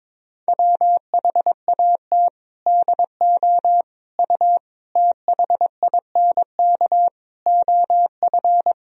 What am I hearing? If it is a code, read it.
Morse code, heard as a keyed tone, about 23 wpm, 700 Hz. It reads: WHAT DO U THINK OF
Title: Sentences from Top 200 Words and Abbreviations 22wpm